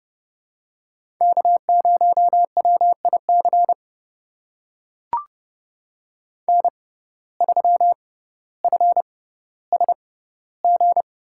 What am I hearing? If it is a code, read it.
Morse code, heard as a keyed tone, about 30 wpm, spaced as for 15 wpm, 700 Hz. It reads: K0WIC N3FSG
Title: USA Call Signs - 5 Characters - 6x Character Spacing - 30wpm